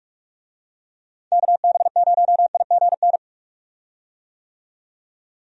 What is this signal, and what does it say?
Morse code, heard as a keyed tone, about 46 wpm, 700 Hz. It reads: KB0IGN
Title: USA Call Signs - 6 Characters - 45wpm